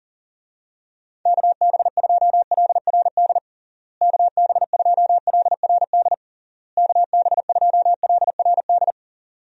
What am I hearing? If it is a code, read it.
Morse code, heard as a keyed tone, about 37 wpm, 700 Hz. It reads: KB2LRD KB2LRD KB2LRD
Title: USA Call Signs - 6 Characters - Repeated 3x - 40wpm